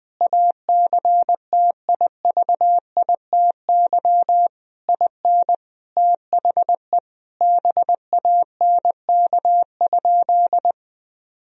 Morse code, heard as a keyed tone, about 20 wpm, 700 Hz. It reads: ACTIVITY IN THE BANK?
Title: Sentences from Top 1000 Words - Review Part 4 - 20wpm